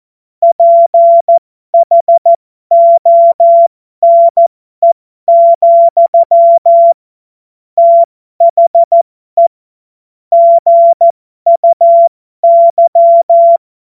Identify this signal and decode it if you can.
Morse code, heard as a keyed tone, about 14 wpm, spaced as for 12 wpm, 680 Hz. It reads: PHONE, THE GUY
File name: Morsekurs / Witze